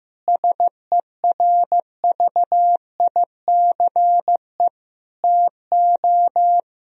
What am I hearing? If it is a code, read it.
Morse code, heard as a keyed tone, about 16 wpm, 700 Hz. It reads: SERVICE TO
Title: Sentences from Top 1000 Words - Review Part 4 - 15wpm